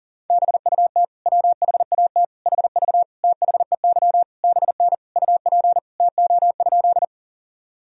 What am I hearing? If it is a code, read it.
Morse code, heard as a keyed tone, about 37 wpm, 700 Hz. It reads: BUT WHAT HV THEY BN UP TO?
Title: Sentences from Top 100 Words and Abbreviations 40wpm